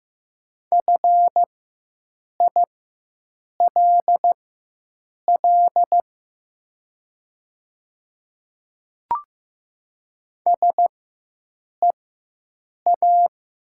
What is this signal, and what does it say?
Morse code, heard as a keyed tone, about 16 wpm, spaced as for 7 wpm, 690 Hz. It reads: FILL SEA
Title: 4x Character Spacing - Top 500 Words - 15wpm